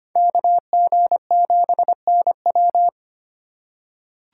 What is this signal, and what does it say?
Morse code, heard as a keyed tone, about 25 wpm, 700 Hz. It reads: KG7NW
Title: USA Call Signs - 5 Characters - 4x Character Spacing - 25wpm